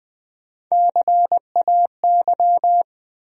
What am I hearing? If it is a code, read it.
Morse code, heard as a keyed tone, about 20 wpm, 700 Hz. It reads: CAY
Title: USA Call Sign Postfixes - 3 Letters 20wpm